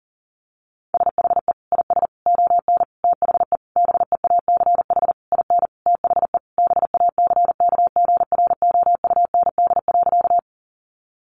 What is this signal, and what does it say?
Morse code, heard as a keyed tone, about 37 wpm, 700 Hz. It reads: SHE IS ON THE BEACH IN THE BACKGROUND.
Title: Sentences from Top 1500 Words 40wpm